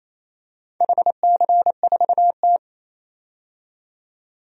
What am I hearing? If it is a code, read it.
Morse code, heard as a keyed tone, about 28 wpm, 700 Hz. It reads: HC4T
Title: DXCC-100 Call Signs - 4 Characters - 28wpm